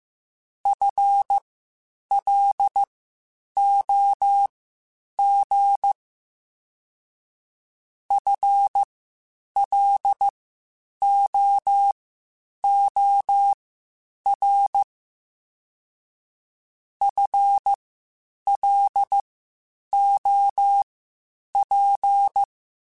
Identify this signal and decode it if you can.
Morse code, heard as a keyed tone, about 15 wpm, spaced as for 8 wpm, 800 Hz. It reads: FLOG FLOOR FLOP
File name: Spaced / FL Runs sp mp3s / FL Runs sp @13